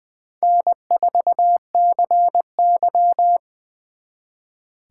Morse code, heard as a keyed tone, about 23 wpm, 700 Hz. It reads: N4CY
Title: Speed Racing - USA 1x2 Call Signs 20wpm